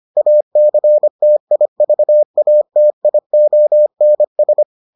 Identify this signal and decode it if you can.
Morse code, heard as a keyed tone, about 25 wpm, 600 Hz. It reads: ACTIVATIONS